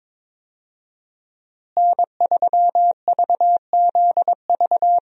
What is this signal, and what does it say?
Morse code, heard as a keyed tone, about 22 wpm, 700 Hz. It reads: N3VZV
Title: USA Call Signs - 5 Characters - Repeated 2x - 22wpm